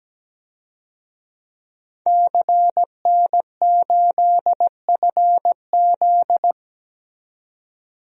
Morse code, heard as a keyed tone, about 17 wpm, 700 Hz. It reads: CN8FZ
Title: DXCC-100 Call Signs - 5 Characters - 17wpm